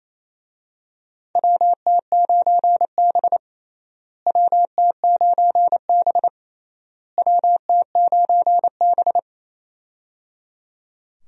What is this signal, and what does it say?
Morse code, heard as a keyed tone, about 28 wpm, 700 Hz. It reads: WT9B WT9B WT9B
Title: USA Call Signs - 4 Characters - Repeated 3x - 28wpm